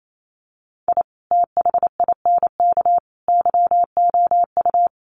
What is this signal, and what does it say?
Morse code, heard as a keyed tone, about 28 wpm, 700 Hz. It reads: I THINK YOU
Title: Sets of 3 Words 28wpm